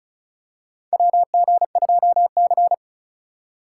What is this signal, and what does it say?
Morse code, heard as a keyed tone, about 36 wpm, 700 Hz. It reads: WG2C
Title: USA Call Signs - 4 Characters - 2x Character Spacing - 35wpm